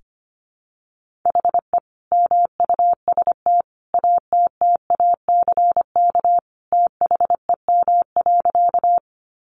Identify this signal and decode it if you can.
Morse code, heard as a keyed tone, about 25 wpm, 700 Hz. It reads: HE MUST ATTACK THEM.
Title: Sentences from Top 1500 Words 25wpm